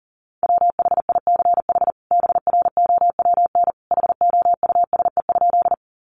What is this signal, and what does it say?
Morse code, heard as a keyed tone, about 40 wpm, 700 Hz. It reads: WHICH BROWN HOUSE?
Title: Sentences from Top 1200 Words 40wpm